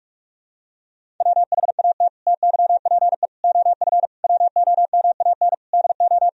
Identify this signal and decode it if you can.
Morse code, heard as a keyed tone, about 46 wpm, 700 Hz. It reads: WHAT TYPE OF WOMAN DO
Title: Sentences from Top 1000 Words - Review Part 3 - 45wpm